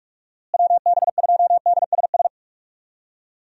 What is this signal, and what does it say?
Morse code, heard as a keyed tone, about 42 wpm, 700 Hz. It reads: WB2DSS
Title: USA Call Signs - 6 Characters - Repeated 2x - 45wpm